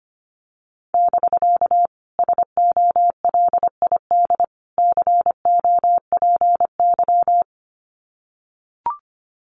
Morse code, heard as a keyed tone, about 25 wpm, 700 Hz. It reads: <BK> SOLID COPY
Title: Common QSO Phrases Encore 25wpm